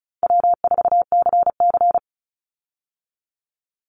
Morse code, heard as a keyed tone, about 36 wpm, 700 Hz. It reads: W4CC
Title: Call Signs - 35wpm